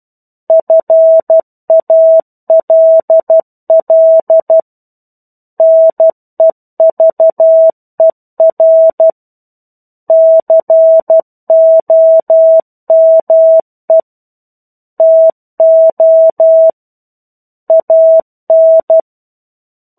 Morse code, heard as a keyed tone, about 12 wpm, 650 Hz. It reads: FALL NEVER COME TO AN
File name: 01-Chapter-01 12 WPM